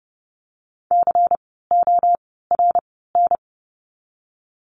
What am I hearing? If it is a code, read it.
Morse code, heard as a keyed tone, about 30 wpm, 700 Hz. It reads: C O R N
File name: Lesson 16f - Letter c - Words 30wpm